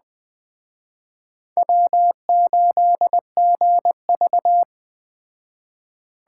Morse code, heard as a keyed tone, about 21 wpm, 700 Hz. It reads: W8GV